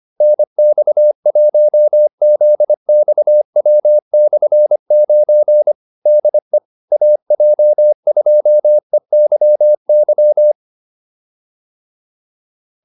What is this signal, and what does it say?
Morse code, heard as a keyed tone, about 25 wpm, 600 Hz. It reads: NX1ZXW/9 DE AJ2EYY
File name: QSO307a 25